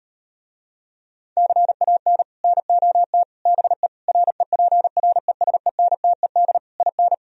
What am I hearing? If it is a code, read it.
Morse code, heard as a keyed tone, about 39 wpm, 700 Hz. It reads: CAN NOT BE REPRESENTED IN